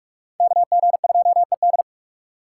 Morse code, heard as a keyed tone, about 56 wpm, 700 Hz. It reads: KG2ED